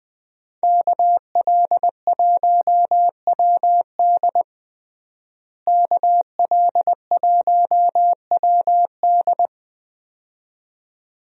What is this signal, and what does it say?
Morse code, heard as a keyed tone, about 20 wpm, 700 Hz. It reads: KL1WD KL1WD
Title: USA Call Signs - 5 Characters - Repeated 2x - 20wpm